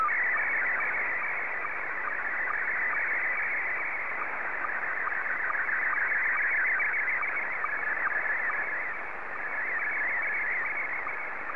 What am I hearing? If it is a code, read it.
SSTV (slow-scan television), Scottie 1.